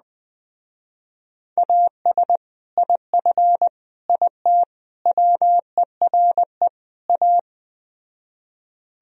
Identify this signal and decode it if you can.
Morse code, heard as a keyed tone, about 20 wpm, 700 Hz. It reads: AS IF IT WERE A
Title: Speed-Racing - Sets of 5 Words 17wpm